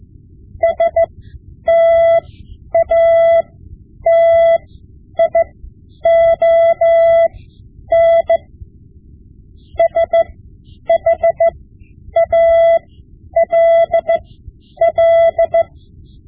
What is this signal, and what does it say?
Morse code, heard as a keyed tone, about 15 wpm, spaced as for 9 wpm, 670 Hz. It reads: STATION SHALL
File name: Morse Lessons / Morse Lessons / Tape-5B